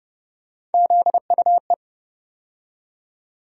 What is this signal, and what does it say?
Morse code, heard as a keyed tone, about 30 wpm, 700 Hz. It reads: ZUE
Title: USA Call Sign Postfixes - 3 Letters 30wpm